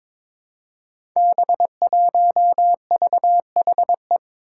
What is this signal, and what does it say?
Morse code, heard as a keyed tone, about 23 wpm, 700 Hz. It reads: B1VHE
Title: Five Challenging Letter-Number Combinations 22wpm